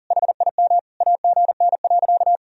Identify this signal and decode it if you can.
Morse code, heard as a keyed tone, about 40 wpm, 700 Hz. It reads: HIM AGN.
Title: Sentences from Top 300 Words and Abbreviations 40wpm